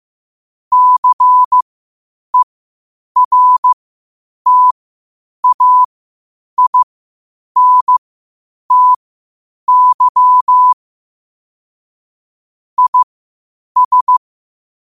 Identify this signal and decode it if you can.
Morse code, heard as a keyed tone, about 15 wpm, spaced as for 8 wpm, 1000 Hz. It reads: CERTAINTY IS
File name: qotdfiles / 8wpm